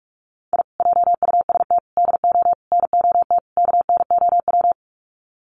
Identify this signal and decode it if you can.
Morse code, heard as a keyed tone, about 44 wpm, 700 Hz. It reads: I JUST DO NOT KNOW
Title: Sets of 5 Words Encore 45wpm